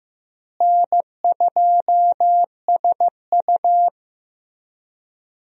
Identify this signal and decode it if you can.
Morse code, heard as a keyed tone, about 15 wpm, 700 Hz. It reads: N2SU
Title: USA Call Signs - 4 Characters - Repeated 3x - 15wpm